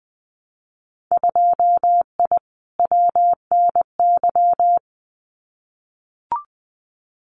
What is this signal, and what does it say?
Morse code, heard as a keyed tone, about 20 wpm, 700 Hz. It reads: 2I WNY
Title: Winter Field Day Exchanges Encore 20wpm